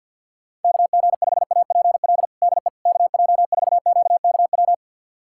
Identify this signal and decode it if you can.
Morse code, heard as a keyed tone, about 52 wpm, 690 Hz. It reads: KG5APL DE KJ4QKW
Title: USA Callsign DE Callsign 50wpm